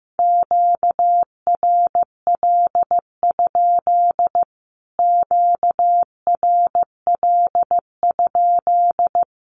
Morse code, heard as a keyed tone, about 15 wpm, 700 Hz. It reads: QRL? QRL?
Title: Q-Codes Repeated 3x 15wpm